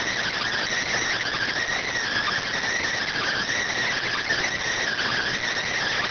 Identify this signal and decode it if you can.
Emergency Alert System (SAME) tones (header bursts), heard as harsh buzzing bursts. It is heard from 1.0 to 5.3 s.